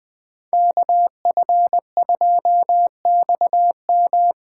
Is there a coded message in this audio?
Yes: Morse code from 0.5 to 4.3 s, heard as a keyed tone, about 20 wpm, 700 Hz. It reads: KF2XM